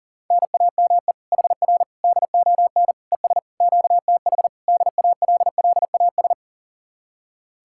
Morse code, heard as a keyed tone, about 52 wpm, 700 Hz. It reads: NAME HR DON ES QTH DALLAS